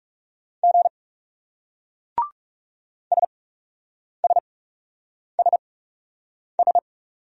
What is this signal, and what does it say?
Morse code, heard as a keyed tone, about 39 wpm, spaced as for 8 wpm, 700 Hz. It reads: G SSSS